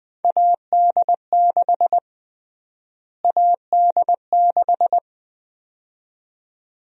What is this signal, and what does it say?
Morse code, heard as a keyed tone, about 20 wpm, 700 Hz. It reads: AD6 AD6
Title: USA Call Sign Prefixes - Repeated 2x - 20wpm